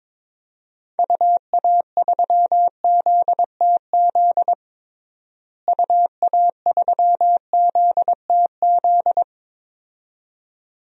Morse code, heard as a keyed tone, about 22 wpm, 700 Hz. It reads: UA3ZTZ UA3ZTZ
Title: DXCC-100 Call Signs - 6 Characters - Repeated 2x - 22wpm